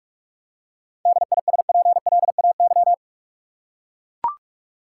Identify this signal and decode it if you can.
Morse code, heard as a keyed tone, about 42 wpm, 700 Hz. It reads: DISPLAY
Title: Seven Letter Words - 45wpm